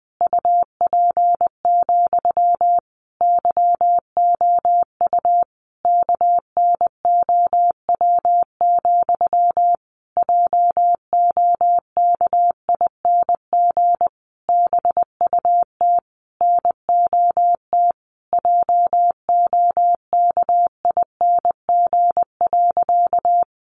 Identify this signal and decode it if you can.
Morse code, heard as a keyed tone, about 19 wpm, 700 Hz. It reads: UP, YOU KNOW, JOKING BUT NOT JOKING.
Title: A Sound Of Thunder 20wpm